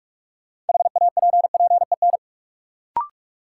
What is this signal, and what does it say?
Morse code, heard as a keyed tone, about 46 wpm, 700 Hz. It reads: HAPPEN
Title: Sets of 5 Words Encore - New Word Drill - 45wpm